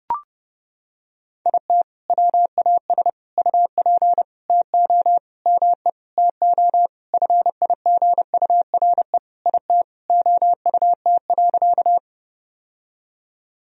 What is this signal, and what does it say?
Morse code, heard as a keyed tone, about 32 wpm, 700 Hz. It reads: IT WAS UP TO ME TO FIGURE IT OUT.
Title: Sentences from Top 1000 Words - Review Part 2 - 30wpm